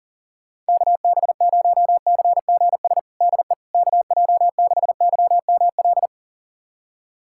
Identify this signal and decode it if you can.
Morse code, heard as a keyed tone, about 37 wpm, 690 Hz. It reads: KB0CGS DE KJ6YML
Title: USA Callsign DE Callsign - 6 Characters - 40wpm